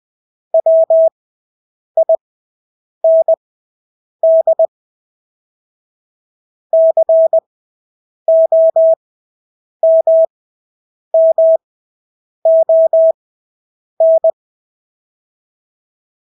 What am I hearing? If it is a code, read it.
Morse code, heard as a keyed tone, about 20 wpm, spaced as for 8 wpm, 650 Hz. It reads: WIND COMMON
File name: Engelske ord 20/8 wpm